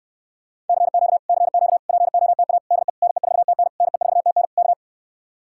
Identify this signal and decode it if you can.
Morse code, heard as a keyed tone, about 68 wpm, 700 Hz. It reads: CQ CQ POTA DE NE4TN NE4TN K